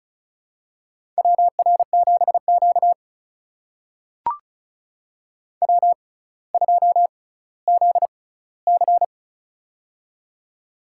Morse code, heard as a keyed tone, about 36 wpm, spaced as for 18 wpm, 700 Hz. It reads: WR7Q W2ZC
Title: USA Call Signs - 4 Characters - 6x Character Spacing - 35wpm